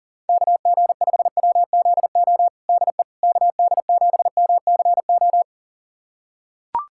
Morse code, heard as a keyed tone, about 40 wpm, 700 Hz. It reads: KG5WZO DE KD7MCO
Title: USA Callsign DE Callsign 40wpm